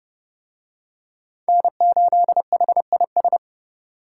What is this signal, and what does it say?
Morse code, heard as a keyed tone, about 26 wpm, 700 Hz. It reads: N8HIS